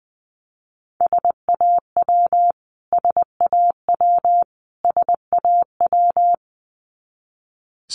Morse code, heard as a keyed tone, about 20 wpm, 700 Hz. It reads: SAW SAW SAW
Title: Top 500 Words Repeated 20wpm